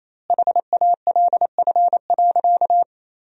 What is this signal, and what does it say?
Morse code, heard as a keyed tone, about 28 wpm, 700 Hz. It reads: HALF.
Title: 3x Word-Spacing - Sentences from Top 400 Words - 28wpm